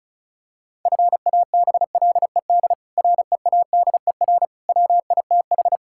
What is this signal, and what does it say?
Morse code, heard as a keyed tone, about 34 wpm, 700 Hz. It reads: FABLED READER WITH
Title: Pangram Challenge v2 - 35wpm